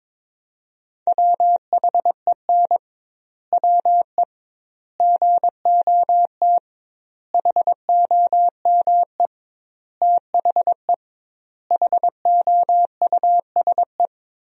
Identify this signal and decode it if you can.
Morse code, heard as a keyed tone, about 22 wpm, 700 Hz. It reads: WHEN WE GOT HOME THE HOUSE
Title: 2x Word-Spacing - Sentences from Top 500 Words - 22wpm